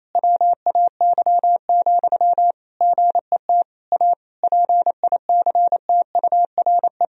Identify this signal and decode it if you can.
Morse code, heard as a keyed tone, about 28 wpm, 700 Hz. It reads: WAY, GET A PICTURE
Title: Sentences from Top 1000 Words - Review Part 1 - 28wpm